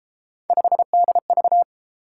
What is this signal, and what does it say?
Morse code, heard as a keyed tone, about 31 wpm, 700 Hz. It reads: 5DV